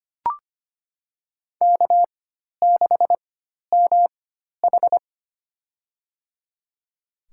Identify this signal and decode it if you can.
Morse code, heard as a keyed tone, about 25 wpm, spaced as for 12 wpm, 700 Hz. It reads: K6MH
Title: USA Call Signs - 4 Characters - 4x Character Spacing - 25wpm